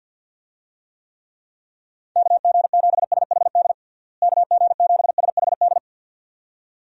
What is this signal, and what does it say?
Morse code, heard as a keyed tone, about 51 wpm, 700 Hz. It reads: KG7SHD KG7SHD
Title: USA Call Signs - 6 Characters - Repeated 2x - 50wpm